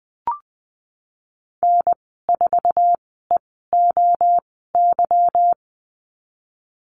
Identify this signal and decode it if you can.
Morse code, heard as a keyed tone, about 20 wpm, spaced as for 15 wpm, 700 Hz. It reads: N4EOY
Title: 147 Lesson 24g - Letter y - Call Signs 20wpm